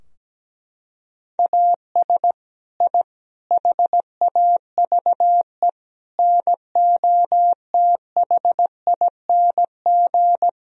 Morse code, heard as a keyed tone, about 17 wpm, 700 Hz. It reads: AS I HAVE NOTHING
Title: Sentences from Top 1000 Words - Review Part 3 - 17wpm